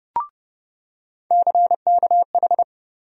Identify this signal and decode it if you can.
Morse code, heard as a keyed tone, about 30 wpm, 700 Hz. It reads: CKH
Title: USA Call Sign Postfixes - 3 Letters 30wpm